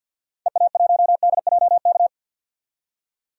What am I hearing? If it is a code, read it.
Morse code, heard as a keyed tone, about 52 wpm, 700 Hz. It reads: EA1DJK
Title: DXCC-100 Call Signs - 6 Characters - 50wpm